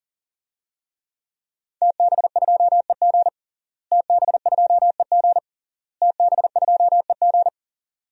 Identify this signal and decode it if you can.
Morse code, heard as a keyed tone, about 40 wpm, 700 Hz. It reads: TB2EG TB2EG TB2EG
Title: DXCC-100 Call Signs - 5 Characters - Repeated 3x - 40wpm